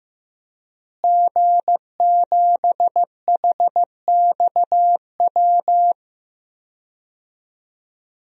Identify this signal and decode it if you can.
Morse code, heard as a keyed tone, about 15 wpm, 700 Hz. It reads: G7HXW
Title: DXCC-100 Call Signs - 5 Characters - Repeated 2x - 15wpm